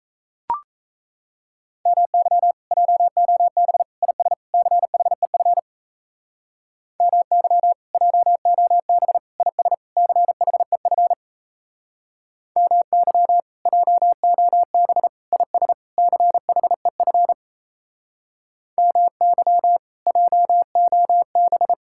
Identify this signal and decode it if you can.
Morse code, heard as a keyed tone, about 38 wpm, 700 Hz. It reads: MY JOB IS CHEF MY JOB IS CHEF MY JOB IS CHEF MY JOB